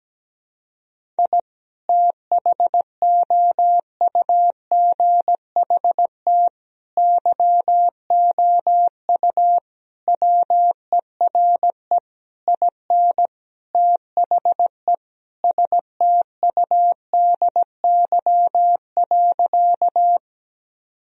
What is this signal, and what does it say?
Morse code, heard as a keyed tone, about 17 wpm, 700 Hz. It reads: I THOUGHT YOU WERE IN THE STUDY.
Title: Sentences from Top 1000 Words - Review Part 1 - 17wpm